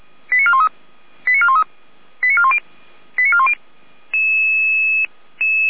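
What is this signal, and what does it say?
Five-tone selective call (ZVEI), heard as quick runs of five tones, tones of about 70 ms. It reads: 89413 89413 89410 89410
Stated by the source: ZVEI five-tone selective call received by SDR, address 89003.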